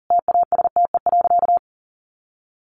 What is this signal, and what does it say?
Morse code, heard as a keyed tone, about 40 wpm, 700 Hz. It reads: TASTE.
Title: Sentences from Top 1100 Words 40wpm